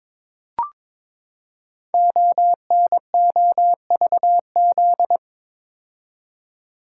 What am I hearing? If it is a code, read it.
Morse code, heard as a keyed tone, about 22 wpm, 700 Hz. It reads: ONOVZ
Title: Code Groups - 5 Letters - 22wpm